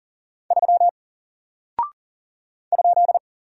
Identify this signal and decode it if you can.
Morse code, heard as a keyed tone, about 39 wpm, 700 Hz. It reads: 3 ?
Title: Single Character (Letter:Number:Punctuation:Prosign) Alt 40wpm